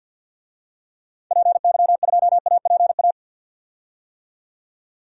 Morse code, heard as a keyed tone, about 51 wpm, 700 Hz. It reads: PY2APA
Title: DXCC-100 Call Signs - 6 Characters - 50wpm